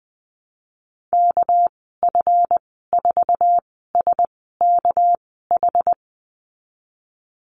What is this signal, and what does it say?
Morse code, heard as a keyed tone, about 20 wpm, spaced as for 15 wpm, 700 Hz. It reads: KF4SKH
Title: 219 Lesson 35g - Number 0 - Call Signs 20wpm